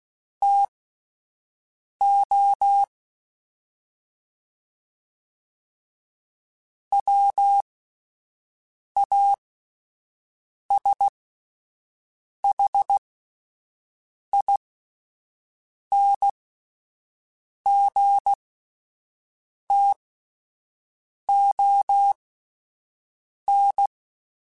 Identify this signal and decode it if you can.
Morse code, heard as a keyed tone, about 16 wpm, spaced as for 5 wpm, 800 Hz. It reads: TO WASHINGTON
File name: (spaced) / QSOs Spaced-mp3s / WD0BGP 1 @10 sp